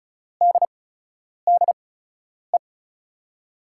Morse code, heard as a keyed tone, about 36 wpm, spaced as for 10 wpm, 700 Hz. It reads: DDE
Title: USA Call Signs - 6 Characters - 8x Character Spacing - 35wpm